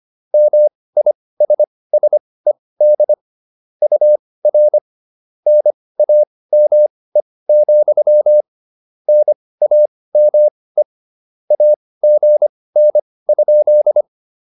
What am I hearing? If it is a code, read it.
Morse code, heard as a keyed tone, about 25 wpm, spaced as for 18 wpm, 600 Hz. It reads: MISSED UR NAME, NAME AGN?
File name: QSO403 18